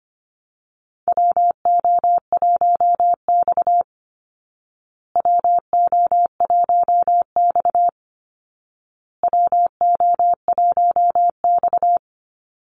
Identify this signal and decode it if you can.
Morse code, heard as a keyed tone, about 25 wpm, 700 Hz. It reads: WO1X WO1X WO1X
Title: Call Signs Repeated 2x - 25wpm